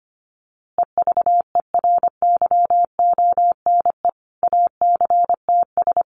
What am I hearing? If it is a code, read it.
Morse code, heard as a keyed tone, about 25 wpm, 700 Hz. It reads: EVERYONE ACTS
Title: Easy Phrases 25wpm